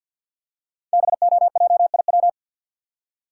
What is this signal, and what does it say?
Morse code, heard as a keyed tone, about 50 wpm, 700 Hz. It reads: BOJIW
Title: Code Groups - 5 Letters - 50wpm